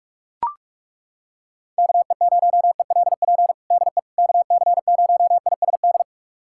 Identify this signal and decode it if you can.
Morse code, heard as a keyed tone, about 41 wpm, 700 Hz. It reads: KE0ELP DE KC0ISD
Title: USA Callsign DE Callsign - 6 Characters - 45wpm